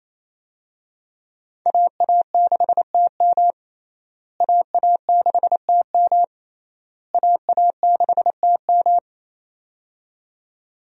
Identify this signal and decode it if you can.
Morse code, heard as a keyed tone, about 27 wpm, 700 Hz. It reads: AA6TM AA6TM AA6TM
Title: USA Call Signs - 5 Characters - Repeated 3x - 28wpm